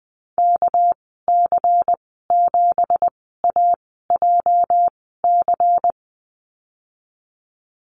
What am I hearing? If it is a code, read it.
Morse code, heard as a keyed tone, about 20 wpm, spaced as for 15 wpm, 700 Hz. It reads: KC7AJC
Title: 219 Lesson 35g - Number 0 - Call Signs 20wpm